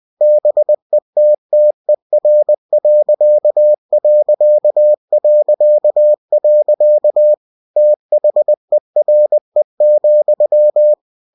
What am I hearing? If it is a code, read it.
Morse code, heard as a keyed tone, about 20 wpm, 600 Hz. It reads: BETTER.... THERE,
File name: cw-books / winnie-the-pooh / winnie-20wpm-page-0054